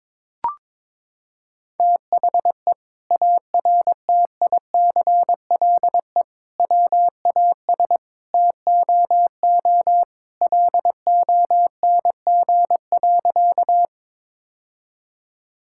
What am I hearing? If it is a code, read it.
Morse code, heard as a keyed tone, about 22 wpm, 700 Hz. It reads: THE ARTICLE WAS TOO LONG.
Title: Sentences from Top 1000 Words - Review Part 5 - 22wpm